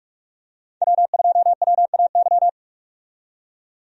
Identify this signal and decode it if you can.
Morse code, heard as a keyed tone, about 47 wpm, 700 Hz. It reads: W2WAY
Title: USA Call Signs - 5 Characters - 45wpm